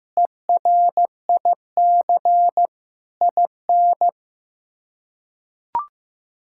Morse code, heard as a keyed tone, about 15 wpm, 700 Hz. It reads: ERIC IN
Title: K1USN SST QSO Practice - 15wpm